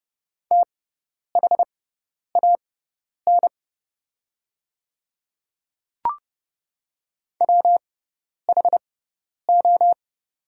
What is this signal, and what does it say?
Morse code, heard as a keyed tone, about 30 wpm, spaced as for 10 wpm, 700 Hz. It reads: THAN WHO
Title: 6x Character Spacing - Top 100 Words - 30wpm